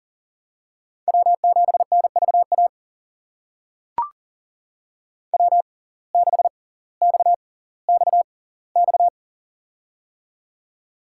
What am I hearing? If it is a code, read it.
Morse code, heard as a keyed tone, about 39 wpm, spaced as for 20 wpm, 700 Hz. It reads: W7NVA W6XXX
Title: USA Call Signs - 5 Characters - 6x Character Spacing - 40wpm